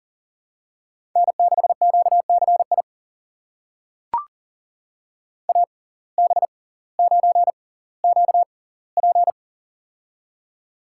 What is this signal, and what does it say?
Morse code, heard as a keyed tone, about 40 wpm, spaced as for 20 wpm, 700 Hz. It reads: N6QCI AB9QP
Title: USA Call Signs - 5 Characters - 6x Character Spacing - 40wpm